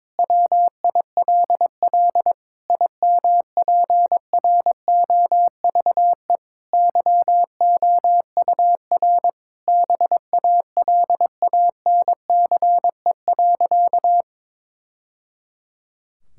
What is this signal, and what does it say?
Morse code, heard as a keyed tone, about 22 wpm, 700 Hz. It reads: WILL IMPROVE YOUR BALANCE.
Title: Sentences from Top 1000 Words - Review Part 6 - 22wpm